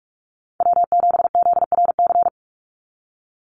Morse code, heard as a keyed tone, about 44 wpm, 700 Hz. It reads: W7ZRC
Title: Call Signs Repeated 3x - 45wpm